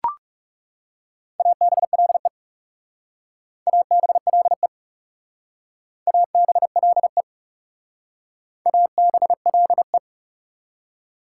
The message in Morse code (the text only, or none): ABLE ABLE ABLE ABLE